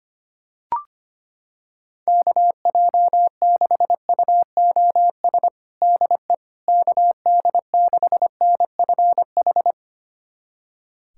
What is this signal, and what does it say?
Morse code, heard as a keyed tone, about 25 wpm, 700 Hz. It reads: KJ6UOS DE KD6NFH
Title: USA Callsign DE Callsign - 6 Characters - 25wpm